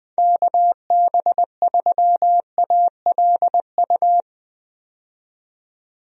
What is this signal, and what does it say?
Morse code, heard as a keyed tone, about 20 wpm, 700 Hz. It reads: KB3ALU
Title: USA Call Signs - 6 Characters - 20wpm